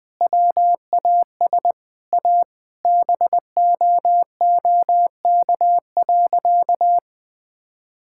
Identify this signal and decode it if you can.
Morse code, heard as a keyed tone, about 20 wpm, 700 Hz. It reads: WAS A BOOK.
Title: Sentences with 4-Word Phrases - 20wpm